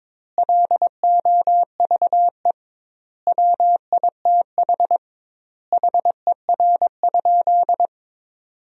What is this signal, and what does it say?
Morse code, heard as a keyed tone, about 22 wpm, 700 Hz. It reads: LOVE WITH HER?
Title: 2x Word-Spacing - Sentences from Top 400 Words - 22wpm